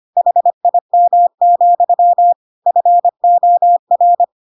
Morse code, heard as a keyed tone, about 25 wpm, 700 Hz. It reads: HIM, FOR
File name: Pinocchio / Pinocchio ch3 25 25 ik1hge